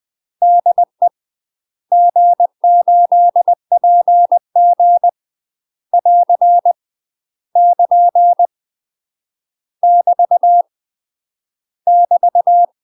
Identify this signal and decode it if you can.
Morse code, heard as a keyed tone, about 20 wpm, 700 Hz. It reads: DE G8PG <AR> <KN> <BT> <BT>
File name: UK MASTER QSOs TRAINING / 20WPM / UK MASTER QSOs 0001